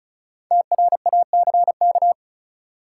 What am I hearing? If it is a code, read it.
Morse code, heard as a keyed tone, about 36 wpm, 700 Hz. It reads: TRACK
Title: Sentences from Top 1100 Words - New Word Drill Alt - 35wpm